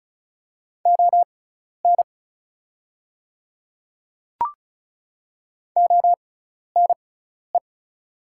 Morse code, heard as a keyed tone, about 36 wpm, spaced as for 12 wpm, 700 Hz. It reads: ON ONE
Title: 6x Character Spacing - Top 100 Words - 35wpm